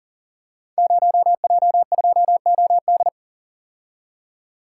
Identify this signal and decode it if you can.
Morse code, heard as a keyed tone, about 41 wpm, 700 Hz. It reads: 0J2OD
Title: Code Groups - 5 Letter:Numbers - 40wpm